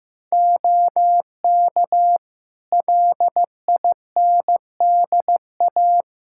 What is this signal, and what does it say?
Morse code, heard as a keyed tone, about 15 wpm, 700 Hz. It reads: OK LINDA
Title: Practice QSO - Exchange 2 - Info - 15wpm